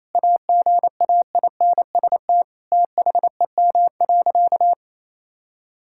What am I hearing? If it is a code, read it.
Morse code, heard as a keyed tone, about 27 wpm, 700 Hz. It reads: AGAINST THEM.
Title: Sentences from Top 1600 Words 28wpm